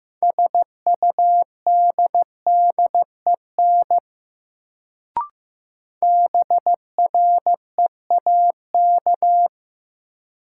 Morse code, heard as a keyed tone, about 15 wpm, 700 Hz. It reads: SUDDEN BREAK
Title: Sets of 4 Words - New Word Drill - 15wpm